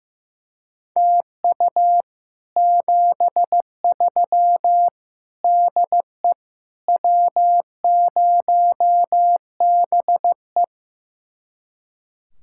Morse code, heard as a keyed tone, about 15 wpm, 700 Hz. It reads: TU 73 DE W0BE
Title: Speed-Racing Common QSO Phrases Encore - 15wpm